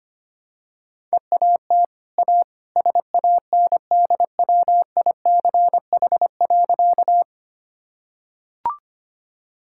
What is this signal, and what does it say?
Morse code, heard as a keyed tone, about 25 wpm, 700 Hz. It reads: EAT A SANDWICH.
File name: Easy Sentences with Hints - 25wpm